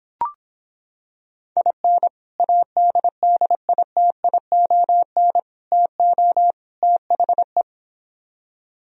Morse code, heard as a keyed tone, about 26 wpm, 700 Hz. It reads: IN ADDITION TO THE